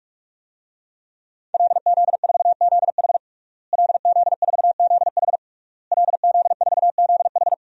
Morse code, heard as a keyed tone, about 41 wpm, 700 Hz. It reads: LZ4ZH LZ4ZH LZ4ZH
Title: DXCC-100 Call Signs - 5 Characters - Repeated 3x - 45wpm